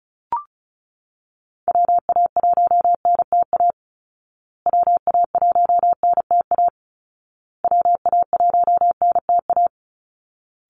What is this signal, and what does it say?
Morse code, heard as a keyed tone, about 34 wpm, 700 Hz. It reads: WA1NTA WA1NTA WA1NTA
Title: Call Signs Repeated 2x - 35wpm